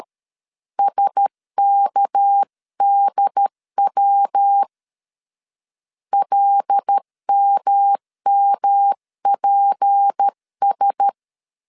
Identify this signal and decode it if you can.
Morse code, heard as a keyed tone, about 13 wpm, 800 Hz. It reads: SKDW LMMPS
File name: CW / DET / 1 6 1500C S C 12wpm 3533